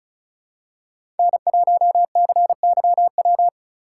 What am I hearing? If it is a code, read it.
Morse code, heard as a keyed tone, about 34 wpm, 690 Hz. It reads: N1CYW
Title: USA Call Signs - 5 Characters - 3x Character Spacing - 35wpm